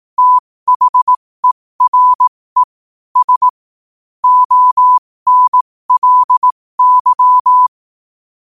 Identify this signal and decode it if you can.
Morse code, heard as a keyed tone, about 18 wpm, spaced as for 15 wpm, 1000 Hz. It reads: THERES ONLY
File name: qotdfiles / 15wpm